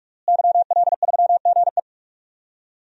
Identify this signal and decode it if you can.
Morse code, heard as a keyed tone, about 46 wpm, 700 Hz. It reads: YL3GE
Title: DXCC-100 Call Signs - 5 Characters - Repeated 3x - 45wpm